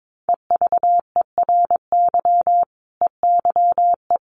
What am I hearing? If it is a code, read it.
Morse code, heard as a keyed tone, about 22 wpm, 700 Hz. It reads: EVERY EYE
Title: Sentences from Top 1100 Words 22wpm